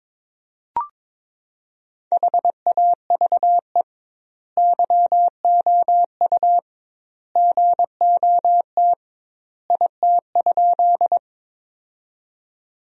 Morse code, heard as a keyed tone, about 22 wpm, 700 Hz. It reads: HAVE YOU GOT IT?
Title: 2x Word-Spacing - Sentences from Top 300 Words - 22wpm